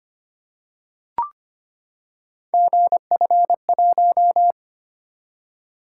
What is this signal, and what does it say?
Morse code, heard as a keyed tone, about 25 wpm, 700 Hz. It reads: GF1
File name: Three Letter-Number Combinations - 25wpm